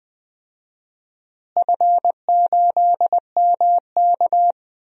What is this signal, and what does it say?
Morse code, heard as a keyed tone, about 20 wpm, 700 Hz. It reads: F8MK
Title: DXCC-100 Call Signs - 4 Characters - Repeated 2x - 20wpm